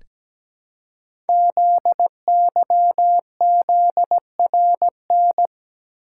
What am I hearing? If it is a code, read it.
Morse code, heard as a keyed tone, about 17 wpm, 700 Hz. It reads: ZYZRN
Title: Code Groups - 5 Letters - 17wpm